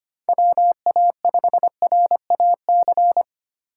Morse code, heard as a keyed tone, about 25 wpm, 700 Hz. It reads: WA5RAC
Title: USA Call Signs - 6 Characters - 2x Character Spacing - 25wpm